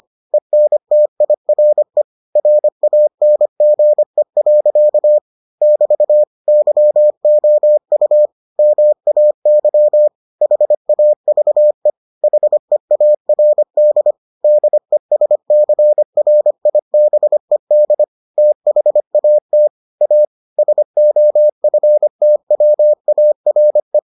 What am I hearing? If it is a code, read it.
Morse code, heard as a keyed tone, about 25 wpm, 600 Hz. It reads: ENTIRE RANGE. <BT> YOU MAY HAVE HEARD DESCRIBED THAT A SOFTWARE